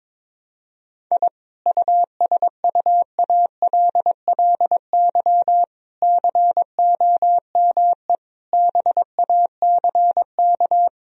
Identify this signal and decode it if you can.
Morse code, heard as a keyed tone, about 22 wpm, 700 Hz. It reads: I USUALLY COME BACK
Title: Sentences from Top 1000 Words - Review Part 3 - 22wpm